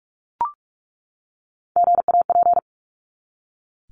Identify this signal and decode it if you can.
Morse code, heard as a keyed tone, about 44 wpm, 700 Hz. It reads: GAP
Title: Top 1501-1750 Words 45wpm